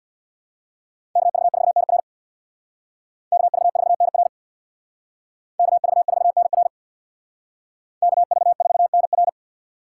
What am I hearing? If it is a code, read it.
Morse code, heard as a keyed tone, about 65 wpm, 690 Hz. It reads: KV4NR KV4NR KV4NR KV4NR